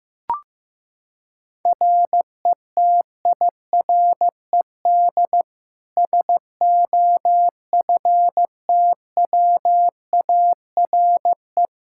Morse code, heard as a keyed tone, about 16 wpm, 700 Hz. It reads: RETIRED SOFTWARE
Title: Practice QSO - Exchange 3 - More Info - 15wpm